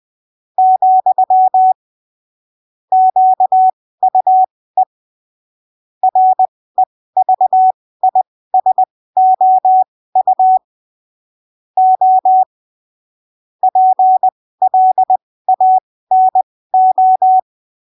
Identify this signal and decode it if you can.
Morse code, heard as a keyed tone, about 20 wpm, spaced as for 15 wpm, 750 Hz. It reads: , QUE REVISOU O PLANO
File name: arquivos / texto / texto 07 20ppm